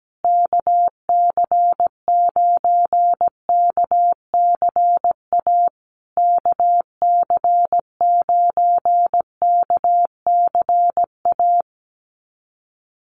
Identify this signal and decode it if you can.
Morse code, heard as a keyed tone, about 17 wpm, 700 Hz. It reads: KC9KCA KC9KCA
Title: USA Call Signs Repeated 3x 17wpm